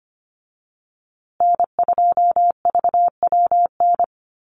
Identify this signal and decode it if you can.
Morse code, heard as a keyed tone, about 25 wpm, 700 Hz. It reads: N2VWN
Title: USA Call Signs Repeated 3x 25wpm